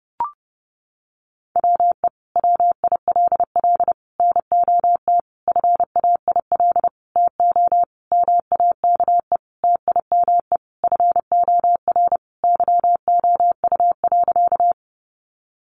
Morse code, heard as a keyed tone, about 30 wpm, 700 Hz. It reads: WE WILL NOT FAIL TO MAKE TIME FOR YOU.
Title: Sentences from Top 1100 Words 30wpm